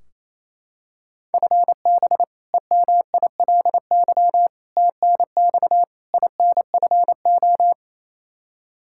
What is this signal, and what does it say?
Morse code, heard as a keyed tone, about 28 wpm, 700 Hz. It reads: FB EMILY TNX INFO
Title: Practice QSO - Exchange 3 - More Info - 28wpm